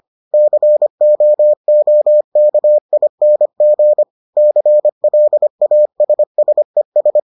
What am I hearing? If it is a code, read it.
Morse code, heard as a keyed tone, about 25 wpm, 600 Hz. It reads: COOKING CLASSES